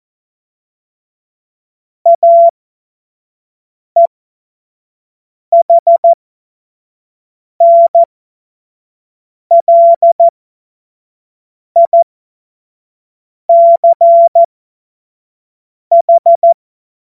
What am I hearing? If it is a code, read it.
Morse code, heard as a keyed tone, about 14 wpm, spaced as for 5 wpm, 680 Hz. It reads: AEHNLICH
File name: Morsekurs / Witze / Witz 18. Januar 2022, 25 F70